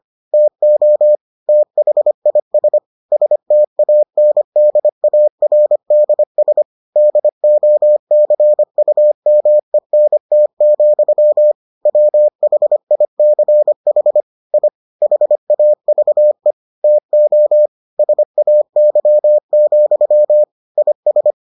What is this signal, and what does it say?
Morse code, heard as a keyed tone, about 25 wpm, 600 Hz. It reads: TO THIS STANDARDS DOCUMENT, WHICH I HAVE TO SAY, IS